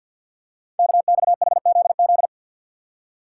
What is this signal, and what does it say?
Morse code, heard as a keyed tone, about 48 wpm, 700 Hz. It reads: KXHZZ
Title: Code Groups - 5 Letters - 50wpm